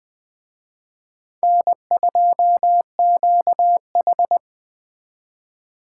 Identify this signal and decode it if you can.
Morse code, heard as a keyed tone, about 21 wpm, 700 Hz. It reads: N2QH
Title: Speed Racing - USA Call Signs 17wpm